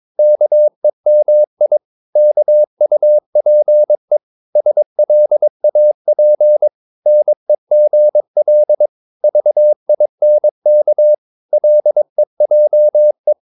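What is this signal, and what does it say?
Morse code, heard as a keyed tone, about 22 wpm, 600 Hz. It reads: KEMI KUPE SLAP NEGL VINK LEJE
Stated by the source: Morse practice of short Danish words at 22 wpm, from CW med Gnister.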